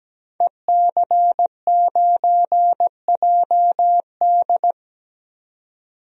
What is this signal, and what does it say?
Morse code, heard as a keyed tone, about 17 wpm, 700 Hz. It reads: EC9JD
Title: DXCC-100 Call Signs - 5 Characters - 17wpm